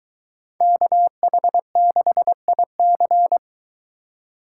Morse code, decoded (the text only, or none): KH6IC